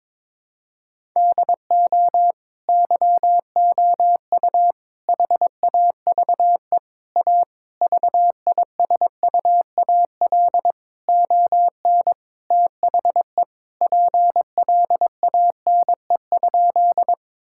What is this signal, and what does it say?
Morse code, heard as a keyed tone, about 22 wpm, 700 Hz. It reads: DO YOU HAVE A VISUAL ON THE PLANE?
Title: Sentences from Top 1000 Words - Review Part 6 - 22wpm